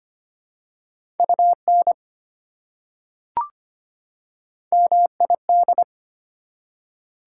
Morse code, heard as a keyed tone, about 25 wpm, 700 Hz. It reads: UN MID
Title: Prefixes and Suffixes 25wpm